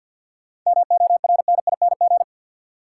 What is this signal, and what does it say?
Morse code, heard as a keyed tone, about 53 wpm, 700 Hz. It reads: MORNING
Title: Sentences from Top 400 Words - New Word Drill - 50wpm